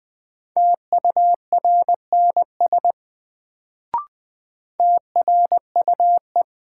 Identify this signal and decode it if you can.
Morse code, heard as a keyed tone, about 21 wpm, 700 Hz. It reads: TURNS TRUE
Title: Sets of 5 Words Encore - New Word Drill - 20wpm